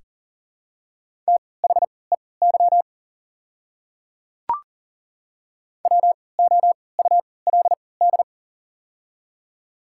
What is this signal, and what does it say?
Morse code, heard as a keyed tone, about 40 wpm, spaced as for 23 wpm, 700 Hz. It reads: THEY WOULD
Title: 3x Character Spacing - Top 200 Words - 40wpm